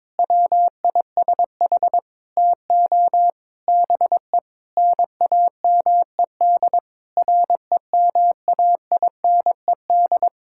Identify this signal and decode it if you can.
Morse code, heard as a keyed tone, about 22 wpm, 700 Hz. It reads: WISH TO BE NAMED REMAINED
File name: Sentences with 7-Word Phrases - 22wpm